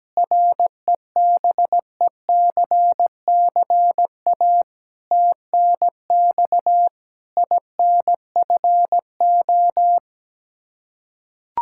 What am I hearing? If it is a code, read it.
Morse code, heard as a keyed tone, about 17 wpm, 700 Hz. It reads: REBECCA TNX INFO
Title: Practice QSO - Exchange 3 - More Info - 17wpm